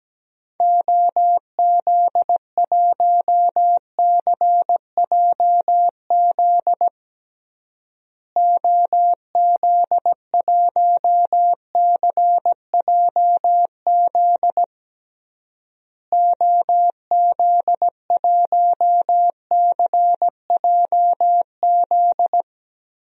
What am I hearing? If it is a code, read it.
Morse code, heard as a keyed tone, about 17 wpm, 700 Hz. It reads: OZ1CJZ OZ1CJZ OZ1CJZ
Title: DXCC-100 Call Signs - 6 Characters - Repeated 3x - 17wpm